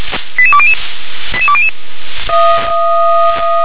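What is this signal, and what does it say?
Five-tone selective call (ZVEI), heard as quick runs of five tones, tones of about 70 ms. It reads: 88200 88200
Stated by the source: ZVEI five-tone selective call 88555, received on an SDR.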